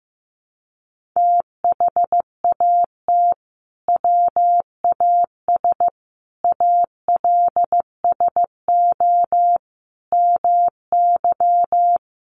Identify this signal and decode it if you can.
Morse code, heard as a keyed tone, about 15 wpm, 700 Hz. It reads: THAT WAS ALSO MY
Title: Sentences from Top 1200 Words 15wpm